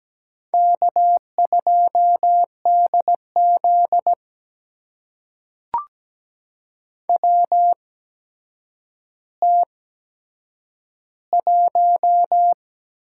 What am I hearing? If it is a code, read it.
Morse code, heard as a keyed tone, about 17 wpm, spaced as for 8 wpm, 700 Hz. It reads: K2DZ WT1
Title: USA Call Signs - 8x Character Spacing - 17wpm